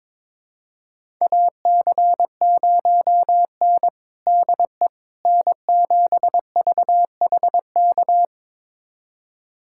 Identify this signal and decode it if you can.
Morse code, heard as a keyed tone, about 22 wpm, 700 Hz. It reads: AC0N DE N7VHK
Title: USA Callsign DE Callsign 22wpm